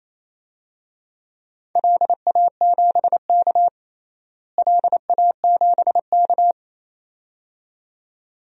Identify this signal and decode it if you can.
Morse code, heard as a keyed tone, about 28 wpm, 700 Hz. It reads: LA7K LA7K
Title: DXCC-100 Call Signs - 4 Characters - Repeated 2x - 28wpm